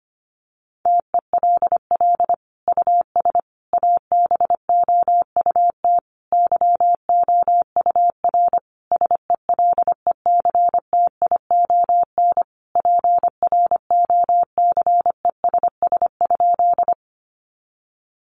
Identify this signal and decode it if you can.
Morse code, heard as a keyed tone, about 25 wpm, 700 Hz. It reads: TELL US ABOUT YOUR SELECTION PROCESS?
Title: Sentences from Top 1400 Words 25wpm